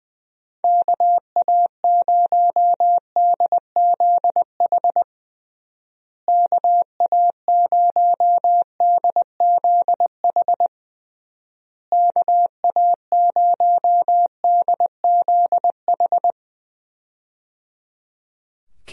Morse code, heard as a keyed tone, about 20 wpm, 700 Hz. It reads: KA0DZH KA0DZH KA0DZH
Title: USA Call Signs - 6 Characters - Repeated 3x - 20wpm